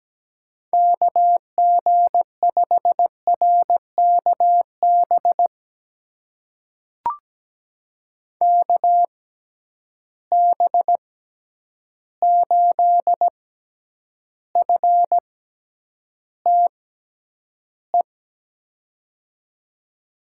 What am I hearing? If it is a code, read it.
Morse code, heard as a keyed tone, about 17 wpm, spaced as for 9 wpm, 700 Hz. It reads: KG5RKB KB8FTE
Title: USA Call Signs - 6 Characters - 6x Character Spacing - 17wpm